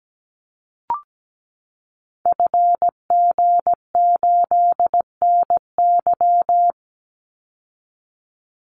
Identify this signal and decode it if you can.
Morse code, heard as a keyed tone, about 17 wpm, 700 Hz. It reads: FG8NY
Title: Call Signs Repeated 2x - 17wpm